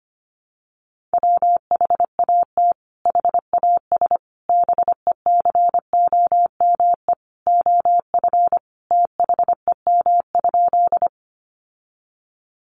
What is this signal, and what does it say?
Morse code, heard as a keyed tone, about 25 wpm, 700 Hz. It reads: WHAT HAS BECOME OF THEM?
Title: Sentences from Top 600 Words 25wpm